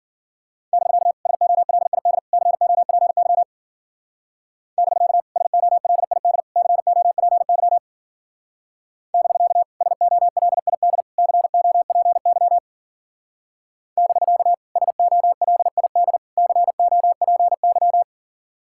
Morse code, heard as a keyed tone, about 52 wpm, 700 Hz. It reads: <BK> SOLID COPY <BK> SOLID COPY <BK> SOLID COPY <BK> SOLID COPY